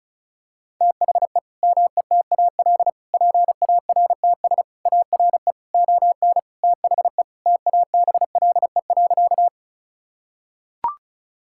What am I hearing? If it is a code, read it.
Morse code, heard as a keyed tone, about 36 wpm, 700 Hz. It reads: THE METAL PARTS ARE ON THE TABLE.
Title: Sentences from Top 1000 Words - Review Part 6 - 35wpm